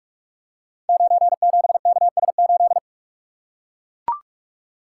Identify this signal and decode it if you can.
Morse code, heard as a keyed tone, about 44 wpm, 690 Hz. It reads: 97KS8
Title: Code Groups - 5 Letter:Numbers - 45wpm